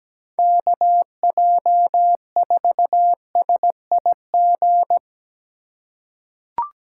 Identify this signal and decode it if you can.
Morse code, heard as a keyed tone, about 17 wpm, 700 Hz. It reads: KJ4SIG
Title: USA Call Signs - 6 Characters - 2x Character Spacing - 17wpm